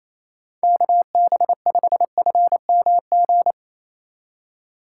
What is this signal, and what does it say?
Morse code, heard as a keyed tone, about 28 wpm, 700 Hz. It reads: KB5FMG
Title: USA Call Signs - 6 Characters - 4x Character Spacing - 28wpm